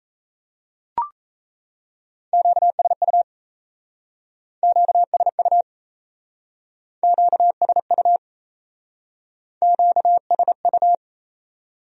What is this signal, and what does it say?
Morse code, heard as a keyed tone, about 35 wpm, 700 Hz. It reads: QSU QSU QSU QSU